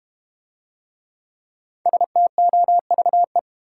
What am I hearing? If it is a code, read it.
Morse code, heard as a keyed tone, about 31 wpm, 700 Hz. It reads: STOVE
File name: Easy Words - Speed-Racing with Hints - 25wpm